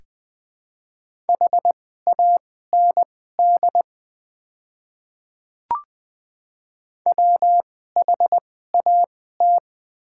Morse code, heard as a keyed tone, about 21 wpm, spaced as for 15 wpm, 700 Hz. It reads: HAND WHAT
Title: 2x Character Spacing - Top 300 Words - 20wpm